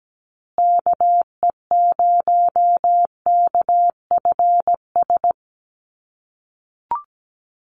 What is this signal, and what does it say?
Morse code, heard as a keyed tone, about 17 wpm, 700 Hz. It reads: KE0KFS
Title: USA Call Signs Repeated 2x 17wpm